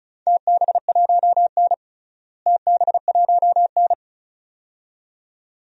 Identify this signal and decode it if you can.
Morse code, heard as a keyed tone, about 36 wpm, 700 Hz. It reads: TB1N TB1N
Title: DXCC-100 Call Signs - 4 Characters - Repeated 2x - 35wpm